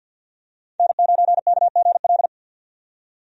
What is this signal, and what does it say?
Morse code, heard as a keyed tone, about 45 wpm, 700 Hz. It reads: N9KGL
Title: USA Call Signs - 5 Characters - 50wpm